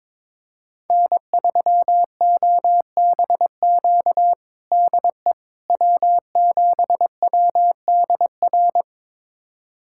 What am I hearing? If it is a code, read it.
Morse code, heard as a keyed tone, about 22 wpm, 700 Hz. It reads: N3OBQ DE W7WDR
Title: USA Callsign DE Callsign - 5 Characters - 22wpm